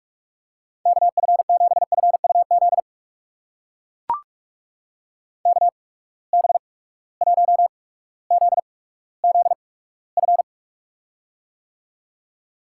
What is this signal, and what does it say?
Morse code, heard as a keyed tone, about 42 wpm, spaced as for 19 wpm, 700 Hz. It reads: KF7FUZ KB1ZZF
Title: USA Call Signs - 6 Characters - 8x Character Spacing - 45wpm